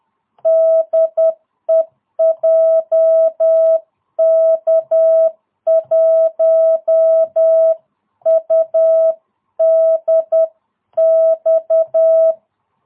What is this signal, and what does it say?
Morse code, heard as a keyed tone, about 10 wpm, 660 Hz. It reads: DE JK1UDX